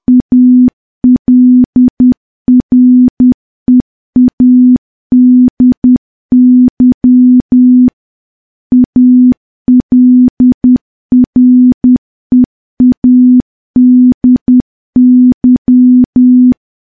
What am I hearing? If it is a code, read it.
Morse code, heard as a keyed tone, about 10 wpm, 250 Hz. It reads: ALREADY ALREADY
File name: morse / morse-training-10WPM